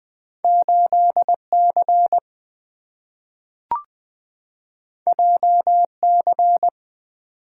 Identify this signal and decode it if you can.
Morse code, heard as a keyed tone, about 20 wpm, 700 Hz. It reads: 8C JC
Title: Two Letter-Number Combinations - 20wpm